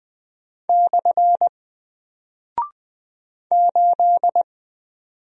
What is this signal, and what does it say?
Morse code, heard as a keyed tone, about 20 wpm, 700 Hz. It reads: / 8
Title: Single Character (Letter:Number:Punctuation:Prosign) Alt 20wpm